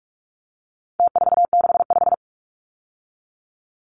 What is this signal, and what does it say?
Morse code, heard as a keyed tone, about 44 wpm, 700 Hz. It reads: T465
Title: Numbers - Contest Style - 4-Digit 45wpm